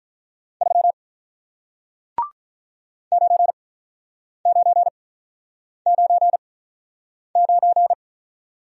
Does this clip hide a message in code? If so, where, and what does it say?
Yes: Morse code from 0.6 to 7.9 s, heard as a keyed tone, about 47 wpm, spaced as for 9 wpm, 700 Hz. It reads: 3 9999